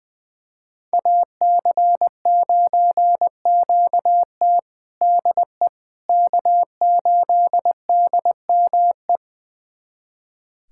Morse code, heard as a keyed tone, about 20 wpm, 700 Hz. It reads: AC9QT DE K8DME
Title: USA Callsign DE Callsign - 5 Characters - 20wpm